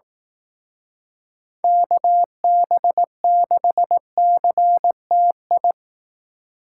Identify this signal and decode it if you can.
Morse code, heard as a keyed tone, about 18 wpm, 700 Hz. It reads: KB6CTI
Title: Speed Racing - USA 2x3 Call Signs 15wpm